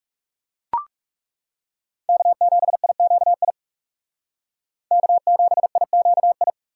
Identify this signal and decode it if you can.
Morse code, heard as a keyed tone, about 36 wpm, 700 Hz. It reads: K7IQI K7IQI
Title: Speed Racing - USA Call Signs 30wpm